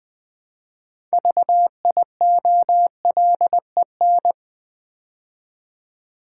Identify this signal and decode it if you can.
Morse code, heard as a keyed tone, about 20 wpm, 700 Hz. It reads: VIOLEN
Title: Build A Word - 2 Letters At A Time - 8-Letter Words - 20wpm